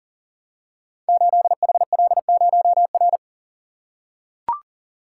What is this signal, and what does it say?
Morse code, heard as a keyed tone, about 40 wpm, 700 Hz. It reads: 8HL0R
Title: Code Groups - 5 Letter:Numbers - 40wpm